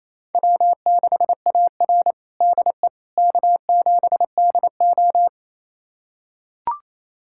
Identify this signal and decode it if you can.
Morse code, heard as a keyed tone, about 28 wpm, 700 Hz. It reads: W6AR DE K7DO
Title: USA Callsign DE Callsign - 4 Characters - 28wpm